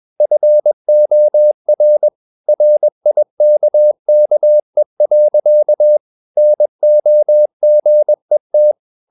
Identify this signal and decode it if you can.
Morse code, heard as a keyed tone, about 21 wpm, 600 Hz. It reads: FOR RIKKE. NOGET
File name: Ferie QRS 21wpm